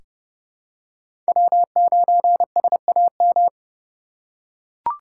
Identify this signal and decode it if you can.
Morse code, heard as a keyed tone, about 32 wpm, 700 Hz. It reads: W9SAM
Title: USA Call Signs - 5 Characters - 4x Character Spacing - 30wpm